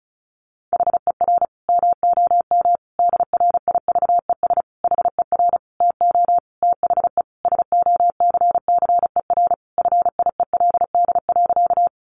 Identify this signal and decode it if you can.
Morse code, heard as a keyed tone, about 34 wpm, 700 Hz. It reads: HER MOM DRIVES HER TO THE SOCCER FIELD.
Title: Easy Phrases 35wpm